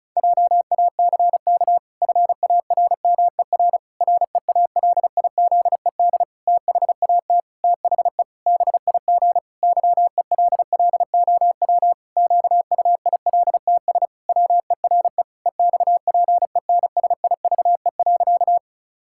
Morse code, heard as a keyed tone, about 36 wpm, 700 Hz. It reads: JACK FARMER REALIZED THAT THE BIG YELLOW QUILTS WERE EXPENSIVE.
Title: Pangram Challenge v2 - 35wpm